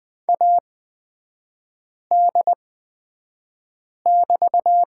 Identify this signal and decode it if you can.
Morse code, heard as a keyed tone, about 20 wpm, spaced as for 5 wpm, 700 Hz. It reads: AD<BT>
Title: 229 Lesson 37e - Letter x - Three Characters 20wpm